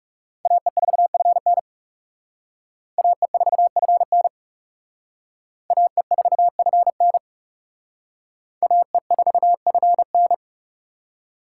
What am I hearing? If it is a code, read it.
Morse code, heard as a keyed tone, about 39 wpm, 700 Hz. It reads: AE4FN AE4FN AE4FN AE4FN